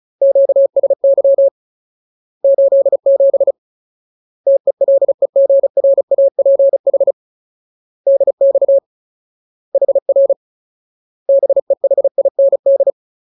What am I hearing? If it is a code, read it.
Morse code, heard as a keyed tone, about 34 wpm, 550 Hz. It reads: QSY 87 TELEGRAPH DX HR BEHIND
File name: Ultimate Word List / 35WPM Ultimate